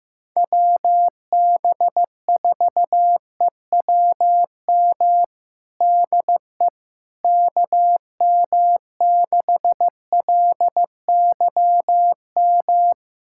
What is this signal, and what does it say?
Morse code, heard as a keyed tone, about 14 wpm, 700 Hz. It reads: WB4EWM DE KM6LYM
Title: USA Callsign DE Callsign 15wpm